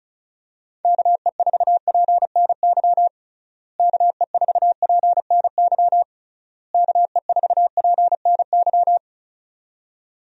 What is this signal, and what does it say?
Morse code, heard as a keyed tone, about 34 wpm, 700 Hz. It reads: KE4PNY KE4PNY KE4PNY
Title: USA Call Signs - 6 Characters - Repeated 3x - 35wpm